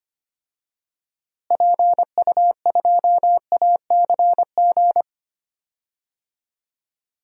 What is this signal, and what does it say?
Morse code, heard as a keyed tone, about 25 wpm, 700 Hz. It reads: PU2ACG
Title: DXCC-100 Call Signs - 6 Characters - Repeated 2x - 25wpm